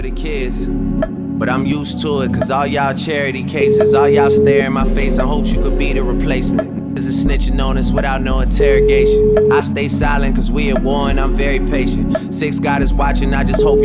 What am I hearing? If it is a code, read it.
Ringback tone (Europe, 425 Hz), with speech.